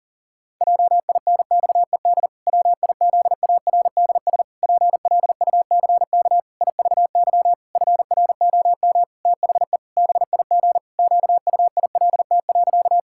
Morse code, heard as a keyed tone, about 40 wpm, 700 Hz. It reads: JINXED WIZARDS PLUCK IVY FROM THE BIG QUILT.
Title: Pangram Challenge v2 - 40wpm